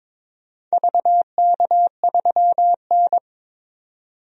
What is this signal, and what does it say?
Morse code, heard as a keyed tone, about 22 wpm, 700 Hz. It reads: VK3N
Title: DXCC-100 Call Signs - 4 Characters - 22wpm